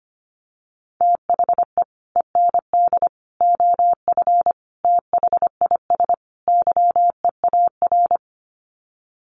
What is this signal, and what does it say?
Morse code, heard as a keyed tone, about 25 wpm, 700 Hz. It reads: THE END OF THIS YEAR
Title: Sets of 5 Words Encore 25wpm